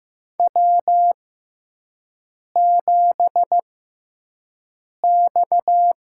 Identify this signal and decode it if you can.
Morse code, heard as a keyed tone, about 15 wpm, spaced as for 5 wpm, 700 Hz. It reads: W7X
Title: USA Call Signs - 6x Character Spacing - 15wpm